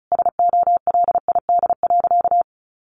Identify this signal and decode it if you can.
Morse code, heard as a keyed tone, about 34 wpm, 700 Hz. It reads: SOLID.
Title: Sentences from Top 1500 Words 35wpm